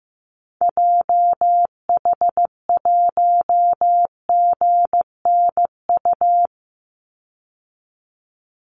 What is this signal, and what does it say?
Morse code, heard as a keyed tone, about 15 wpm, 700 Hz. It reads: JH1GNU
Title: Call Signs Repeated 2x - 15wpm